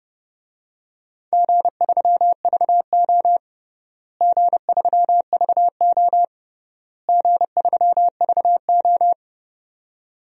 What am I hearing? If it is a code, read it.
Morse code, heard as a keyed tone, about 30 wpm, 700 Hz. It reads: G3VO G3VO G3VO
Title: DXCC-100 Call Signs - 4 Characters - Repeated 3x - 30wpm